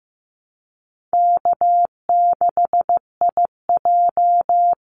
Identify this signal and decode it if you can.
Morse code, heard as a keyed tone, about 15 wpm, 700 Hz. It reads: K6IJ
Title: Easy Call Signs Repeated 3x - 15wpm